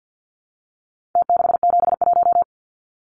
Morse code, heard as a keyed tone, about 50 wpm, 700 Hz. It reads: T671
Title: Numbers - Contest Style - 4-Digit 50wpm